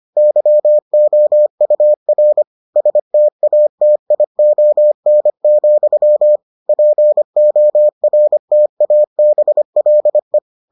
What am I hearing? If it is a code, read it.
Morse code, heard as a keyed tone, about 25 wpm, 600 Hz. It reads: YOUR STATION, PORTABLE